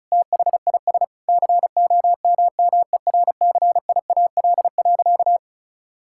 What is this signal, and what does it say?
Morse code, heard as a keyed tone, about 34 wpm, 700 Hz. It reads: THIS COMMERCIAL.
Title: Sentences from Top 1000 Words - Review Part 5 - 35wpm